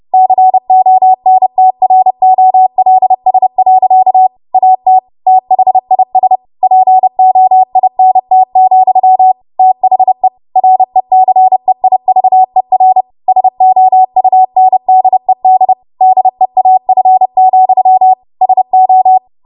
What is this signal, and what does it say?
Morse code, heard as a keyed tone, about 30 wpm, 750 Hz. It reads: CONTROLS. AT THIS POINT, THE RECEIVER SOUNDED DEAF, SO
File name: Archive / 30 WPM / 30WPM